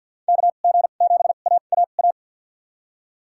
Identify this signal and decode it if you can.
Morse code, heard as a keyed tone, about 48 wpm, spaced as for 35 wpm, 700 Hz. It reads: KG7AAA
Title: USA Call Signs - 2x Character Spacing - 50wpm